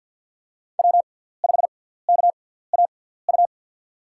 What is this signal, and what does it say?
Morse code, heard as a keyed tone, about 51 wpm, spaced as for 17 wpm, 700 Hz. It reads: W5KAU
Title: USA Call Signs - 5 Characters - 6x Character Spacing - 50wpm